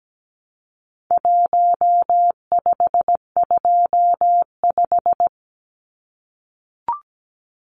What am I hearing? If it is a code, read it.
Morse code, heard as a keyed tone, about 17 wpm, 700 Hz. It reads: 1525
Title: Numbers - Contest Style - 4-Digit 17wpm